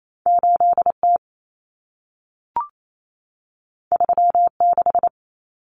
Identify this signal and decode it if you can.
Morse code, heard as a keyed tone, about 28 wpm, 700 Hz. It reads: 8T 36
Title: Common Cut Numbers - 2-Digit 28wpm